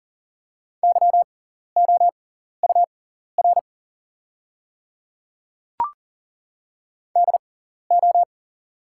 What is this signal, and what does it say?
Morse code, heard as a keyed tone, about 40 wpm, spaced as for 14 wpm, 690 Hz. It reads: YOUR DO
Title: 6x Character Spacing - Top 300 Words - 40wpm